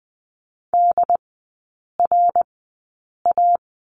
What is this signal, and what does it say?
Morse code, heard as a keyed tone, about 20 wpm, spaced as for 8 wpm, 700 Hz. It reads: DRA
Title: 067 Lesson 12e - Letter d - Three Characters 20wpm